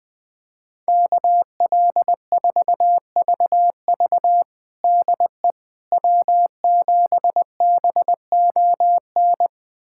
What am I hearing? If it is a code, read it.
Morse code, heard as a keyed tone, about 20 wpm, 700 Hz. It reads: KL4VV DE W7BON
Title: USA Callsign DE Callsign - 5 Characters - 20wpm